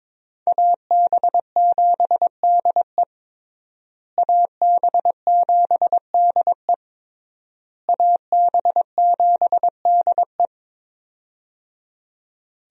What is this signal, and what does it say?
Morse code, heard as a keyed tone, about 22 wpm, 700 Hz. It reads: AB7DE AB7DE AB7DE
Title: USA Call Signs - 5 Characters - Repeated 3x - 22wpm